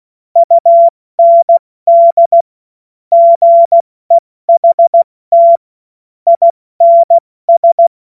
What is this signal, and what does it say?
Morse code, heard as a keyed tone, about 16 wpm, 680 Hz. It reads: UND GEHT INS